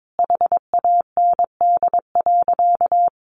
Morse code, heard as a keyed tone, about 22 wpm, 700 Hz. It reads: HAND.
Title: Sentences from Top 1200 Words 22wpm